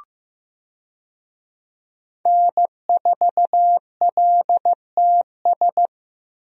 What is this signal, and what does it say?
Morse code, heard as a keyed tone, about 15 wpm, 700 Hz. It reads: N4LTS
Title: USA Call Signs - 5 Characters - Repeated 2x - 15wpm